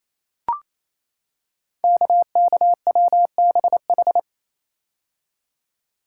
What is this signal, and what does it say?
Morse code, heard as a keyed tone, about 28 wpm, 700 Hz. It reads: KKWBH
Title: Code Groups - 5 Letters - 28wpm